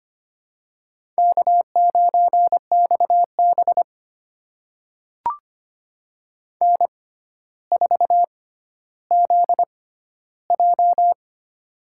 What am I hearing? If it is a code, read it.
Morse code, heard as a keyed tone, about 25 wpm, spaced as for 13 wpm, 700 Hz. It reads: K9XB N4ZJ
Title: USA Call Signs - 4 Characters - 6x Character Spacing - 25wpm